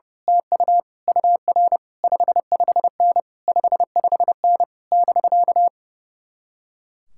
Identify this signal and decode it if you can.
Morse code, heard as a keyed tone, about 28 wpm, 700 Hz. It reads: TU UR 55N 55N <BK>
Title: Speed-Racing Common QSO Phrases - 30wpm